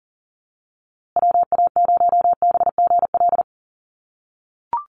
Morse code, heard as a keyed tone, about 40 wpm, 700 Hz. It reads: WA0BGL
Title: USA Call Signs Repeated 3x 40wpm